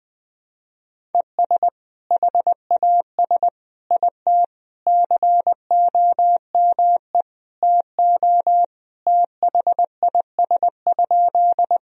Morse code, heard as a keyed tone, about 20 wpm, 700 Hz. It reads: ES HAS IT COME TO THIS?
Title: Sentences from Top 100 Words and Abbreviations 20wpm